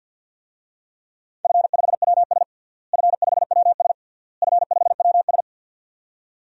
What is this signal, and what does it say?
Morse code, heard as a keyed tone, about 53 wpm, 700 Hz. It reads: F5WS F5WS F5WS
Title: DXCC-100 Call Signs - 4 Characters - Repeated 3x - 50wpm